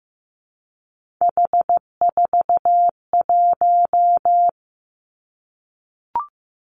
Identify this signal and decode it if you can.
Morse code, heard as a keyed tone, about 15 wpm, 700 Hz. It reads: H41
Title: Three Challenging Letter-Number Combinations 15wpm